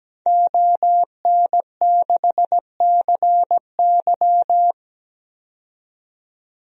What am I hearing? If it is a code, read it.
Morse code, heard as a keyed tone, about 17 wpm, 700 Hz. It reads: ON6CY
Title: DXCC-100 Call Signs - 5 Characters - 17wpm